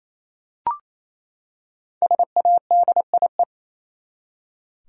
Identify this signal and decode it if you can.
Morse code, heard as a keyed tone, about 27 wpm, 700 Hz. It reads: SADIE
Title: USA Common Names - Max 5 Letters - 28wpm